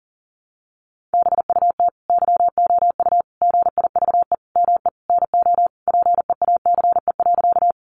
Morse code, heard as a keyed tone, about 40 wpm, 700 Hz. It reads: BUT YOU GIVE ME NO PEACE.
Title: Sentences from Top 1500 Words 40wpm